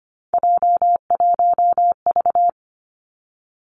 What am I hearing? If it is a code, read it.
Morse code, heard as a keyed tone, about 25 wpm, 700 Hz. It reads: J1V
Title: Three Challenging Letter-Number Combinations 25wpm